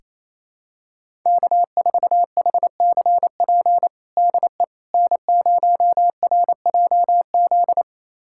Morse code, heard as a keyed tone, about 28 wpm, 700 Hz. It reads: K4HCP DE N0RJZ
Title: USA Callsign DE Callsign - 5 Characters - 28wpm